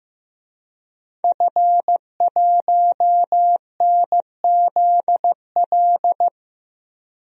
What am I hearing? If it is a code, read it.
Morse code, heard as a keyed tone, about 15 wpm, 700 Hz. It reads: F1NZL
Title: DXCC-100 Call Signs - 5 Characters - Repeated 3x - 15wpm